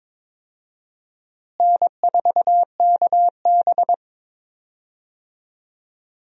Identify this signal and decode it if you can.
Morse code, heard as a keyed tone, about 22 wpm, 700 Hz. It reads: N4KB
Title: USA Call Signs - 4 Characters - Repeated 2x - 22wpm